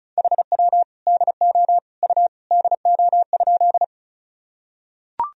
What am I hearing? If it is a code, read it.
Morse code, heard as a keyed tone, about 34 wpm, 700 Hz. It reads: HW DO U DO?
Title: Sentences from Top 100 Words and Abbreviations 35wpm